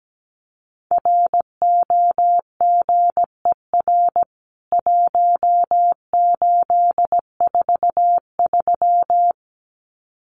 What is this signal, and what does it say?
Morse code, heard as a keyed tone, about 17 wpm, 700 Hz. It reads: ROGER 1843
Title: CWT Exchange 17wpm